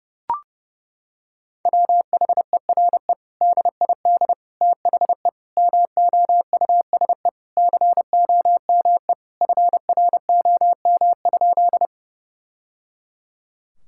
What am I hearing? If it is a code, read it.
Morse code, heard as a keyed tone, about 30 wpm, 700 Hz. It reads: WHERE DID THE MOUSE COME FROM?
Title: Sentences from Top 1000 Words - Review Part 5 - 30wpm